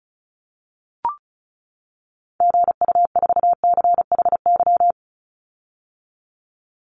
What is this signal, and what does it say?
Morse code, heard as a keyed tone, about 36 wpm, 700 Hz. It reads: GU4CHY
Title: Call Signs - 35wpm